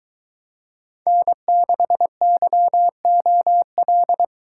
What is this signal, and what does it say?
Morse code, heard as a keyed tone, about 23 wpm, 700 Hz. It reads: N6YOL